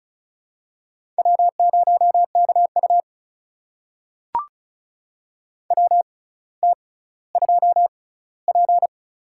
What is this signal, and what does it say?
Morse code, heard as a keyed tone, about 36 wpm, spaced as for 18 wpm, 700 Hz. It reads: W0KU WT2P
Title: USA Call Signs - 4 Characters - 6x Character Spacing - 35wpm